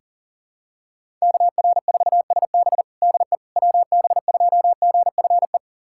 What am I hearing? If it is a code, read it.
Morse code, heard as a keyed tone, about 40 wpm, 700 Hz. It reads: KR4SB DE WB2GFE
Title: USA Callsign DE Callsign 40wpm